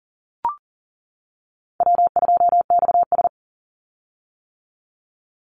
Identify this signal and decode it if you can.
Morse code, heard as a keyed tone, about 39 wpm, 700 Hz. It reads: W2XS
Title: Easy Call Signs - 40wpm